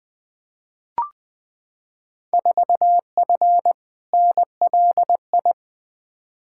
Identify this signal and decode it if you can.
Morse code, heard as a keyed tone, about 20 wpm, 700 Hz. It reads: 4F NLI
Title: Field Day Exchanges Encore - 20wpm